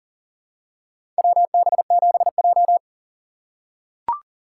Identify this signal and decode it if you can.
Morse code, heard as a keyed tone, about 40 wpm, 700 Hz. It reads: WB7J
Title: USA Call Signs - 4 Characters - 4x Character Spacing - 40wpm